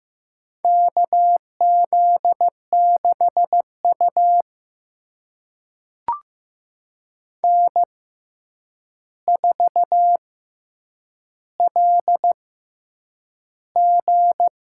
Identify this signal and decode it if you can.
Morse code, heard as a keyed tone, about 15 wpm, spaced as for 8 wpm, 700 Hz. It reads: KZ6U N4LG
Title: USA Call Signs - 4 Characters - 6x Character Spacing - 15wpm